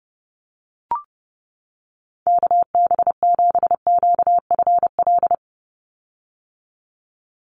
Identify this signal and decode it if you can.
Morse code, heard as a keyed tone, about 30 wpm, 700 Hz. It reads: KB7QFL
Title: USA Call Signs 30wpm